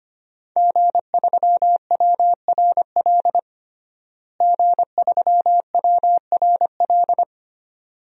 Morse code, heard as a keyed tone, about 25 wpm, 700 Hz. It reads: G3WRL G3WRL
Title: DXCC-100 Call Signs - 5 Characters - Repeated 2x - 25wpm